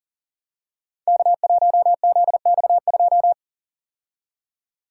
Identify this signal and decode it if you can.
Morse code, heard as a keyed tone, about 41 wpm, 700 Hz. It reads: K1ZX2
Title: Code Groups - 5 Letter:Numbers - 40wpm